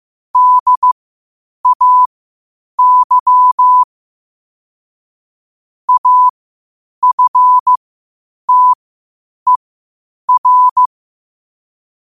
Morse code, heard as a keyed tone, about 15 wpm, spaced as for 8 wpm, 1000 Hz. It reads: DAY AFTER
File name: qotdfiles / 8wpm